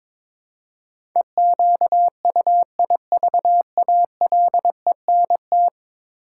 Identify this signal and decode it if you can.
Morse code, heard as a keyed tone, about 21 wpm, 700 Hz. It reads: EQUIVALENT
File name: practice / 22wpm / 10 Letter Words - First 3 Letters Given - 22wpm